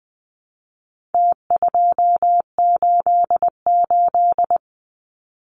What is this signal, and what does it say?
Morse code, heard as a keyed tone, about 20 wpm, 700 Hz. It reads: T288
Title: Numbers - Contest Style - 4-Digit 20wpm